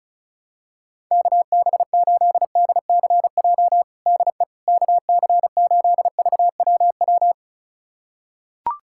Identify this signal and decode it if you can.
Morse code, heard as a keyed tone, about 36 wpm, 700 Hz. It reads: KB8DCJ DE KC8VWW
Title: USA Callsign DE Callsign - 6 Characters - 35wpm